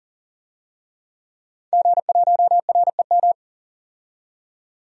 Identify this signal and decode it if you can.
Morse code, heard as a keyed tone, about 40 wpm, 700 Hz. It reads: G1REM
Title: DXCC-100 Call Signs - 5 Characters - 40wpm